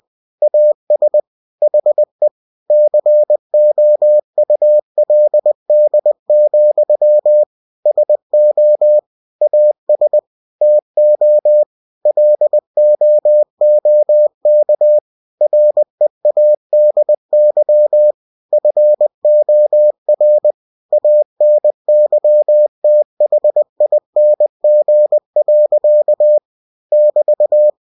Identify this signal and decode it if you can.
Morse code, heard as a keyed tone, about 20 wpm, 600 Hz. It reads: AS HE COULD, SO AS TO LOOK READY FOR ANYTHING. <BT>
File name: cw-books / winnie-the-pooh / winnie-20wpm-page-0076